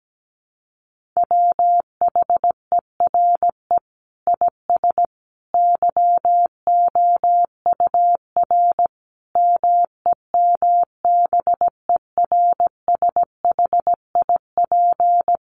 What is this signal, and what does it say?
Morse code, heard as a keyed tone, about 17 wpm, 700 Hz. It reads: WHERE IS YOUR MEMBERSHIP
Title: Sentences from Top 1400 Words 17wpm